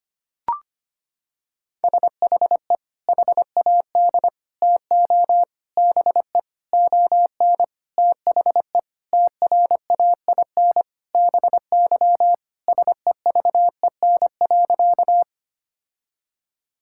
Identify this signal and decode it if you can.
Morse code, heard as a keyed tone, about 25 wpm, 700 Hz. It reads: SHE HAD TO BE ON THE TRAIN BY SEVEN.
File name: practice / 25wpm / Sentences with 4-Word Phrases - 25wpm